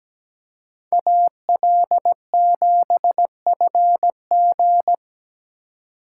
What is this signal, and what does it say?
Morse code, heard as a keyed tone, about 20 wpm, 700 Hz. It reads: AL7FG
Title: Speed Racing - USA 2x2 Call Signs 17wpm